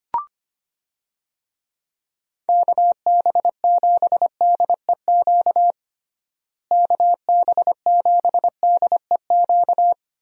Morse code, heard as a keyed tone, about 25 wpm, 700 Hz. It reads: KB7DEQ KB7DEQ
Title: USA Call Signs - 6 Characters - Repeated 2x - 25wpm